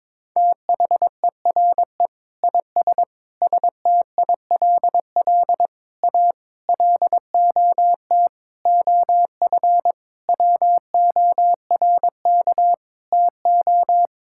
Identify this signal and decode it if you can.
Morse code, heard as a keyed tone, about 22 wpm, 700 Hz. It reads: THERE IS STILL A LOT OF WORK TO
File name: Sentences with 9-Word Phrases - 22wpm